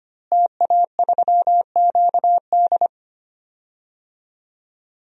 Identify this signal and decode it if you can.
Morse code, heard as a keyed tone, about 25 wpm, 700 Hz. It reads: TA3QD
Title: DXCC-100 Call Signs - 5 Characters - Repeated 3x - 25wpm